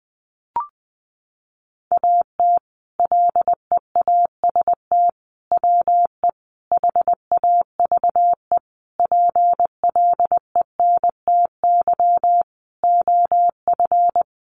Morse code, heard as a keyed tone, about 20 wpm, 700 Hz. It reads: AT LEAST WE HAVE PLENTY OF
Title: Sentences from Top 1300 Words 20wpm